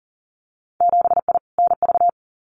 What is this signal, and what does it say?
Morse code, heard as a keyed tone, about 40 wpm, 700 Hz. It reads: 7I NV
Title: Winter Field Day Exchanges Encore 40wpm